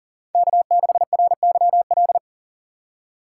Morse code, heard as a keyed tone, about 40 wpm, 700 Hz. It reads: K6RYL
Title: USA Call Signs - 5 Characters - 3x Character Spacing - 40wpm